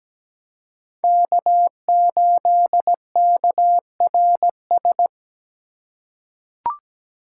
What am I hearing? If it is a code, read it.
Morse code, heard as a keyed tone, about 17 wpm, 700 Hz. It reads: K8KRS
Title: USA Call Signs - 5 Characters - 2x Character Spacing - 17wpm